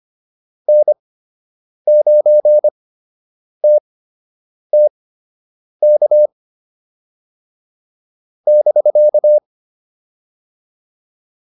Morse code, heard as a keyed tone, about 25 wpm, spaced as for 8 wpm, 600 Hz. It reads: N9TTK <BK>